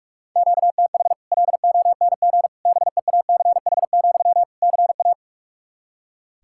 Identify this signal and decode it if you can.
Morse code, heard as a keyed tone, about 47 wpm, 700 Hz. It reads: QTH LONG BEACH, CA
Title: Practice QSO - Exchange 1 - Beginning - 45wpm